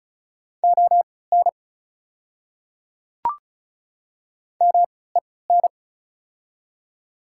Morse code, heard as a keyed tone, about 36 wpm, spaced as for 20 wpm, 690 Hz. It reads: ON MEN
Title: 3x Character Spacing - Top 200 Words - 35wpm